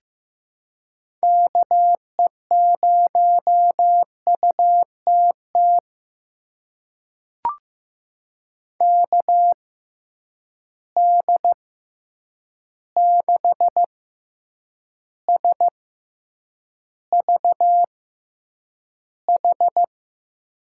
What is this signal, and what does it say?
Morse code, heard as a keyed tone, about 15 wpm, spaced as for 8 wpm, 700 Hz. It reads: KE0UTT KD6SVH
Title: USA Call Signs - 6 Characters - 6x Character Spacing - 15wpm